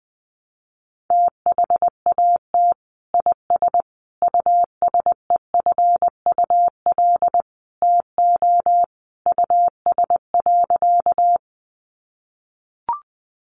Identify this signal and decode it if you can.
Morse code, heard as a keyed tone, about 20 wpm, 700 Hz. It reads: THAT IS USEFUL TO US.
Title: Sentences from Top 700 Words 20wpm